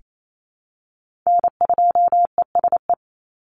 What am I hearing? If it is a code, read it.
Morse code, heard as a keyed tone, about 28 wpm, 700 Hz. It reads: N2ESE
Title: NA SOTA Chaser Callsigns 28wpm